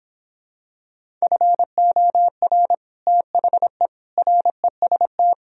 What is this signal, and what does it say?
Morse code, heard as a keyed tone, about 26 wpm, 700 Hz. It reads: FOR THE REST